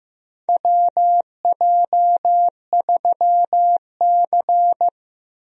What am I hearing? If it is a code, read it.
Morse code, heard as a keyed tone, about 15 wpm, 700 Hz. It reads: WJ3C
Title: USA Call Signs - 4 Characters - Repeated 2x - 15wpm